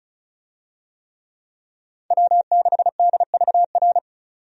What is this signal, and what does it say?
Morse code, heard as a keyed tone, about 36 wpm, 700 Hz. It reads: W6DVR
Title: USA Call Signs - 5 Characters - 35wpm